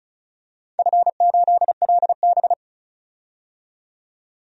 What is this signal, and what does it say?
Morse code, heard as a keyed tone, about 34 wpm, 700 Hz. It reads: F8LB
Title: DXCC-100 Call Signs - 4 Characters - 35wpm